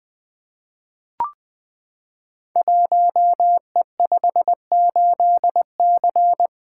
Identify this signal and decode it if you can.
Morse code, heard as a keyed tone, about 20 wpm, 700 Hz. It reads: 1E58C
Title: Code Groups - 5 Letter:Numbers - 20wpm